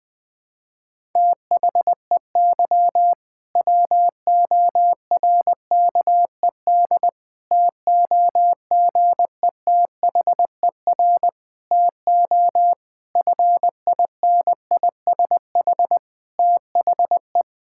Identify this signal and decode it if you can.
Morse code, heard as a keyed tone, about 20 wpm, 700 Hz. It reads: THEY WORKED TOGETHER TO FINISH THE
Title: Difficult Sentences with Hints - 20wpm